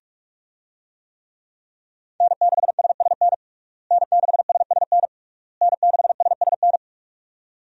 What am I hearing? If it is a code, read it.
Morse code, heard as a keyed tone, about 41 wpm, 700 Hz. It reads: N6SSN N6SSN N6SSN
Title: USA Call Signs - 5 Characters - Repeated 3x - 45wpm